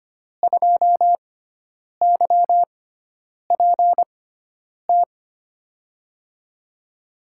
Morse code, heard as a keyed tone, about 25 wpm, spaced as for 9 wpm, 700 Hz. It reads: 2YPT
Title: USA Call Signs - 5 Characters - 6x Character Spacing - 25wpm